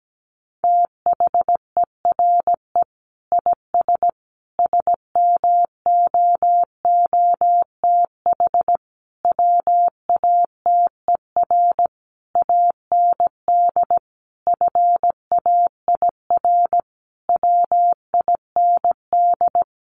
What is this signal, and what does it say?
Morse code, heard as a keyed tone, about 17 wpm, 700 Hz. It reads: THERE IS SMOOTH WATER AND FAIR WIND
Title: Sentences from Top 1400 Words 17wpm